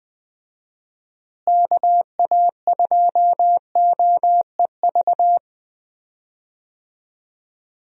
Morse code, heard as a keyed tone, about 21 wpm, 700 Hz. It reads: KA2OEV
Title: USA Call Signs - 6 Characters - Repeated 2x - 20wpm